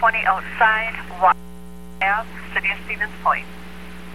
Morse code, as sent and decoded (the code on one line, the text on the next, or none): none